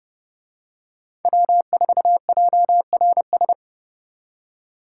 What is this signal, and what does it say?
Morse code, heard as a keyed tone, about 29 wpm, 700 Hz. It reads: W4JRS